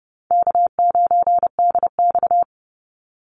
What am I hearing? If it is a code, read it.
Morse code, heard as a keyed tone, about 30 wpm, 700 Hz. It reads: K9DX
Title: Easy Call Signs Repeated 2x - 30wpm